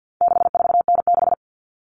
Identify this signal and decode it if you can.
Morse code, heard as a keyed tone, about 50 wpm, 700 Hz. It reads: 64N6
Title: All Cut Numbers - 4-Digit 50wpm